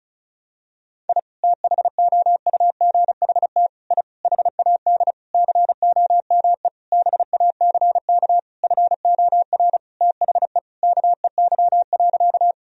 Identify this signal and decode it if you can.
Morse code, heard as a keyed tone, about 36 wpm, 700 Hz. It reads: I THOUGHT I HAD COME BACK FOR THE KEY.
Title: Sentences from Top 1000 Words - Review Part 4 - 35wpm